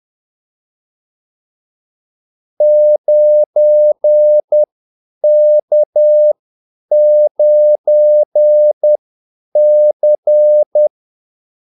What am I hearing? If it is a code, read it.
Morse code, heard as a keyed tone, about 10 wpm, spaced as for 8 wpm, 600 Hz. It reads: 9K9C
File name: Callsigns-10-8wpm